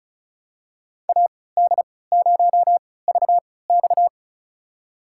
Morse code, heard as a keyed tone, about 36 wpm, spaced as for 20 wpm, 690 Hz. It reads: AD0VX
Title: USA Call Signs - 5 Characters - 3x Character Spacing - 35wpm